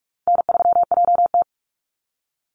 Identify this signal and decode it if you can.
Morse code, heard as a keyed tone, about 44 wpm, 700 Hz. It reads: N3JT
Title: Call Signs Repeated 2x - 45wpm